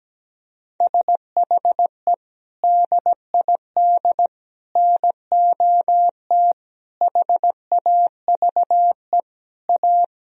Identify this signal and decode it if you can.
Morse code, heard as a keyed tone, about 17 wpm, 700 Hz. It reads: SHE DID NOT HAVE A
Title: Sentences from Top 1000 Words - Review Part 2 - 17wpm